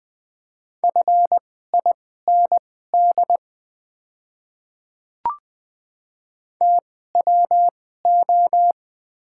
Morse code, heard as a keyed tone, about 20 wpm, spaced as for 14 wpm, 700 Hz. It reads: FIND TWO
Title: 2x Character Spacing - Top 100 Words - 20wpm